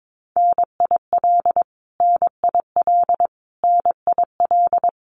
Morse code, heard as a keyed tone, about 22 wpm, 700 Hz. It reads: NIL NIL NIL
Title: Abbreviations Repeated 3x - 22wpm